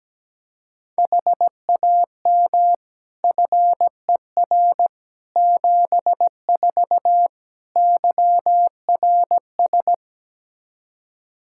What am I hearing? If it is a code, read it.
Morse code, heard as a keyed tone, about 17 wpm, 700 Hz. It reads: HAM FER 74 YRS
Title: Practice QSO - Exchange 3 - More Info - 17wpm